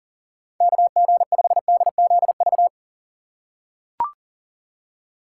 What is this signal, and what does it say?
Morse code, heard as a keyed tone, about 37 wpm, 700 Hz. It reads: KG5DZV
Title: USA Call Signs - 3x Character Spacing - 40wpm